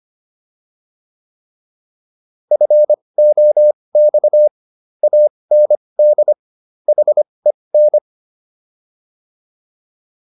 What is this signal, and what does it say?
Morse code, heard as a keyed tone, about 25 wpm, spaced as for 20 wpm, 600 Hz. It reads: FOX AND HEN